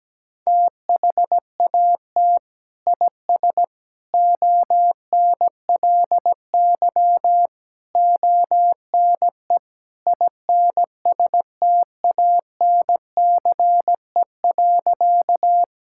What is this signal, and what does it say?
Morse code, heard as a keyed tone, about 17 wpm, 700 Hz. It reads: THAT IS ONLY ONE INSTANCE.
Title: Sentences from Top 1000 Words - Review Part 5 - 17wpm